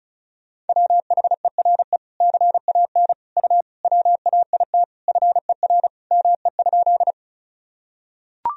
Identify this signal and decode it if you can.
Morse code, heard as a keyed tone, about 34 wpm, 700 Hz. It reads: WHERE CAN U WAIT FER ME?
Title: Sentences from Top 500 Words and Abbreviations 35wpm